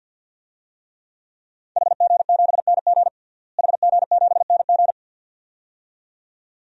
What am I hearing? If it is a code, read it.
Morse code, heard as a keyed tone, about 51 wpm, 700 Hz. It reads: HG7NG HG7NG
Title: DXCC-100 Call Signs - 5 Characters - Repeated 2x - 50wpm